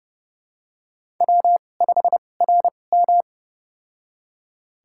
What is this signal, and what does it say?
Morse code, heard as a keyed tone, about 30 wpm, spaced as for 22 wpm, 690 Hz. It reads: W5RM
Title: USA Call Signs - 4 Characters - 2x Character Spacing - 30wpm